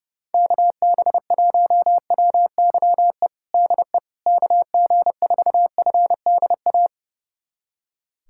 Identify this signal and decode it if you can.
Morse code, heard as a keyed tone, about 30 wpm, 700 Hz. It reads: KB1WYE DE KG4FDA
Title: USA Callsign DE Callsign - 6 Characters - 30wpm